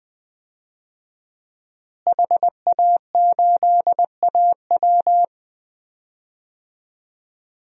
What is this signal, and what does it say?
Morse code, heard as a keyed tone, about 20 wpm, 700 Hz. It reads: HA8AW
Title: DXCC-100 Call Signs - 5 Characters - Repeated 3x - 20wpm